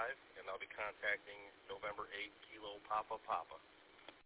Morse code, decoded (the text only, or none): none